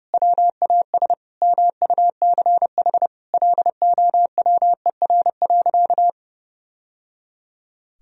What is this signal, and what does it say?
Morse code, heard as a keyed tone, about 30 wpm, 700 Hz. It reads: WAS MUCH LOWER.
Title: Sentences from Top 1000 Words - Review Part 4 - 30wpm